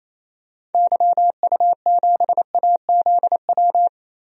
Morse code, heard as a keyed tone, about 27 wpm, 700 Hz. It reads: YU7AZW
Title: DXCC-100 Call Signs - 6 Characters - 28wpm